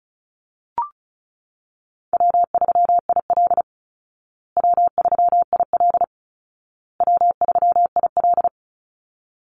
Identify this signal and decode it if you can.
Morse code, heard as a keyed tone, about 34 wpm, 700 Hz. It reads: W3IL W3IL W3IL
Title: Easy Call Signs Repeated 3x - 35wpm